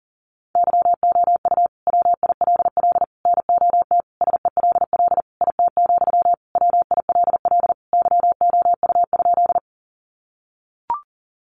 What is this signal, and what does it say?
Morse code, heard as a keyed tone, about 40 wpm, 700 Hz. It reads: YOU WILL NOT SELL IT, WILL YOU?
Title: Sentences from Top 1100 Words 40wpm